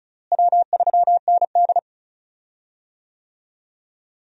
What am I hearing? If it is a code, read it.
Morse code, heard as a keyed tone, about 36 wpm, 700 Hz. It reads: W3ND
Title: USA Call Signs - 4 Characters - Repeated 3x - 35wpm